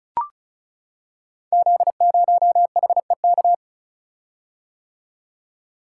Morse code, heard as a keyed tone, about 36 wpm, 700 Hz. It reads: Z0HEK
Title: Code Groups - 5 Letter:Numbers - 35wpm